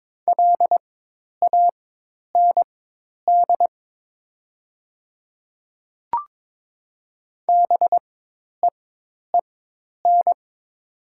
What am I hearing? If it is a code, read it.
Morse code, heard as a keyed tone, about 22 wpm, spaced as for 10 wpm, 690 Hz. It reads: LAND BEEN
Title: 4x Character Spacing - Top 200 Words - 22wpm